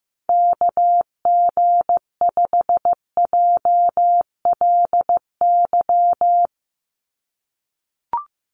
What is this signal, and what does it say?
Morse code, heard as a keyed tone, about 15 wpm, 700 Hz. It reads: KG5JLY
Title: USA Call Signs 15wpm